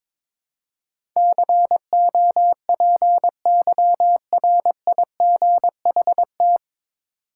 Morse code, heard as a keyed tone, about 22 wpm, 700 Hz. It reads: COPYRIGHT
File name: 9 Letter Words - First 3 Letters Given - 22wpm